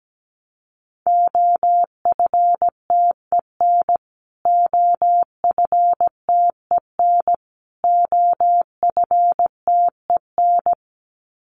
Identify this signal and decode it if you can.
Morse code, heard as a keyed tone, about 17 wpm, 700 Hz. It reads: OFTEN OFTEN OFTEN
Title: Top 500 Words Repeated 17wpm